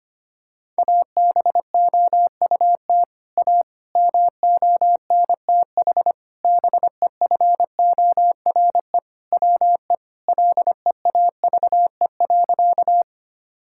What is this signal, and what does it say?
Morse code, heard as a keyed tone, about 25 wpm, 700 Hz. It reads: ABOUT A MONTH BEFORE WE LEAVE.
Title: Sentences from Top 1000 Words - Review Part 5 - 25wpm